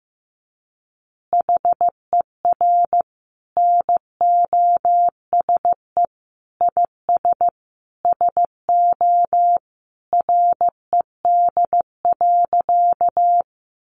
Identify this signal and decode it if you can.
Morse code, heard as a keyed tone, about 15 wpm, 700 Hz. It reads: HER NOSE IS SO RED.
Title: Easy Phrases 15wpm